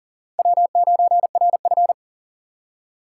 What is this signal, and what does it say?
Morse code, heard as a keyed tone, about 40 wpm, 700 Hz. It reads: W9RF
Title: USA Call Signs - 4 Characters - 6x Character Spacing - 40wpm